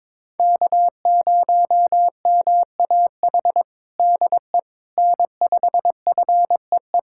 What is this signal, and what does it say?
Morse code, heard as a keyed tone, about 21 wpm, 700 Hz. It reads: K0MAH DE N5FEE
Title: USA Callsign DE Callsign 22wpm